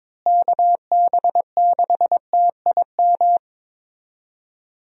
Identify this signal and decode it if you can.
Morse code, heard as a keyed tone, about 21 wpm, 700 Hz. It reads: KB6TIM
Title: USA Call Signs - 6 Characters - Repeated 3x - 22wpm